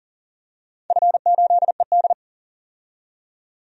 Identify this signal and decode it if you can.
Morse code, heard as a keyed tone, about 40 wpm, 700 Hz. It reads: F8ED
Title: DXCC-100 Call Signs - 4 Characters - 40wpm